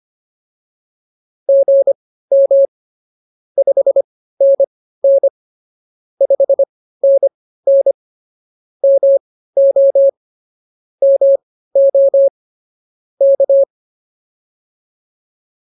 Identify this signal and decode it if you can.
Morse code, heard as a keyed tone, about 24 wpm, spaced as for 15 wpm, 550 Hz. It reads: GM 5NN 5NN MO MO K
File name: POTA201 15